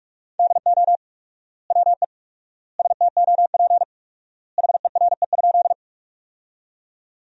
Morse code, heard as a keyed tone, about 42 wpm, 700 Hz. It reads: DO WE STOP HERE?
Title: 4x Word-Spacing - Sentences from Top 300 Words - 45wpm